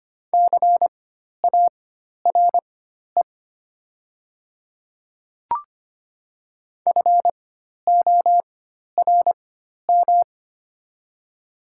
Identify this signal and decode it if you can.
Morse code, heard as a keyed tone, about 25 wpm, spaced as for 12 wpm, 700 Hz. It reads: CARE FORM
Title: 4x Character Spacing - Top 500 Words - 25wpm